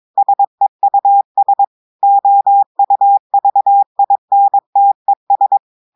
Morse code, heard as a keyed tone, about 22 wpm, 800 Hz. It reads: SEUS OUVINTES
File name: audios / portugais / portugais texte en clair 2 vitesse 22 mots minutes